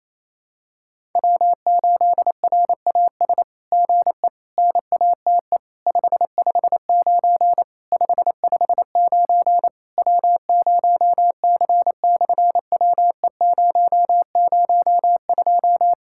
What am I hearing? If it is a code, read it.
Morse code, heard as a keyed tone, about 28 wpm, 700 Hz. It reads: W8RAS GE NATE 559 559 W0C/WE002
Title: SOTA QSO Practice 28wpm